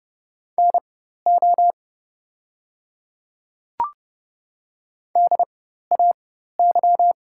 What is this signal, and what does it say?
Morse code, heard as a keyed tone, about 30 wpm, spaced as for 14 wpm, 700 Hz. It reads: NO DAY
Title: 4x Character Spacing - Top 200 Words - 30wpm